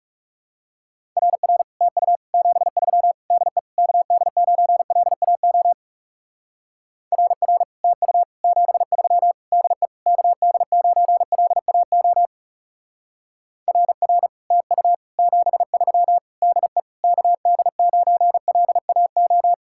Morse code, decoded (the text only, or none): RR TU 73 DE KD9LAO RR TU 73 DE KD9LAO RR TU 73 DE KD9LAO